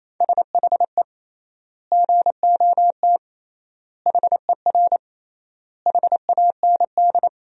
Morse code, heard as a keyed tone, about 28 wpm, 700 Hz. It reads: SHE GOT HER HAND
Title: 3x Word-Spacing - Sentences from Top 500 Words - 28wpm